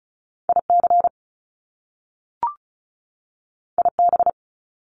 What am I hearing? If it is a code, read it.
Morse code, heard as a keyed tone, about 36 wpm, 700 Hz. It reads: IC IB
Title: Top Two Letter Combinations in Words 35wpm